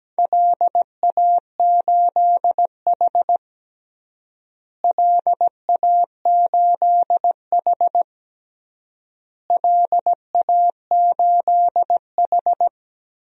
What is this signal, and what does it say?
Morse code, heard as a keyed tone, about 17 wpm, 700 Hz. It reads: LA8H LA8H LA8H
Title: DXCC-100 Call Signs - 4 Characters - Repeated 3x - 17wpm